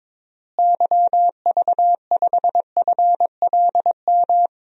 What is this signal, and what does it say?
Morse code, heard as a keyed tone, about 22 wpm, 700 Hz. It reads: YV5FLM
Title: DXCC-100 Call Signs - 6 Characters - Repeated 3x - 22wpm